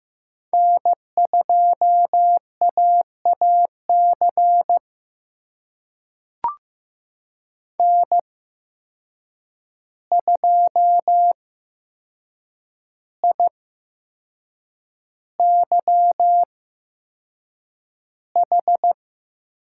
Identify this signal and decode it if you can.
Morse code, heard as a keyed tone, about 15 wpm, spaced as for 6 wpm, 700 Hz. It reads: N2AAC N2IYH
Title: USA Call Signs - 5 Characters - 8x Character Spacing - 15wpm